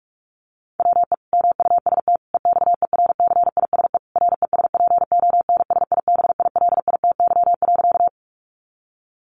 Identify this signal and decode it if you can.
Morse code, heard as a keyed tone, about 44 wpm, 700 Hz. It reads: WE MUST EXERCISE RESPONSIBILITY.
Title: Sentences from Top 1200 Words 45wpm